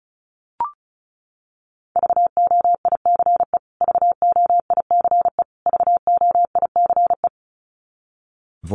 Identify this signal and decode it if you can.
Morse code, heard as a keyed tone, about 34 wpm, 700 Hz. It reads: VOICE VOICE VOICE
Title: Top 500 Words Repeated 35wpm